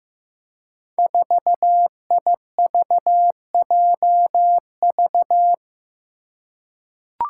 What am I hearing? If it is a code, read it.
Morse code, heard as a keyed tone, about 15 wpm, 700 Hz. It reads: 4IVJV
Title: Five Challenging Letter-Number Combinations 15wpm